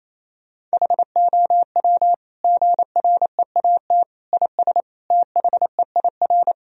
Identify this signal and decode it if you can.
Morse code, heard as a keyed tone, about 28 wpm, 700 Hz. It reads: HOW GREAT IS THEIR
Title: Sentences from Top 1000 Words - Review Part 4 - 28wpm